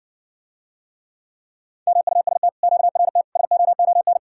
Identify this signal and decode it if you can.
Morse code, heard as a keyed tone, about 60 wpm, 700 Hz. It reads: MUST QRT SOON